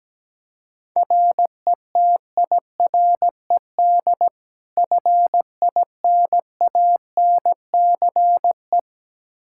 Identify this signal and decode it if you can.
Morse code, heard as a keyed tone, about 17 wpm, 700 Hz. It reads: RETIRED FINANCE
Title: Practice QSO - Exchange 3 - More Info - 17wpm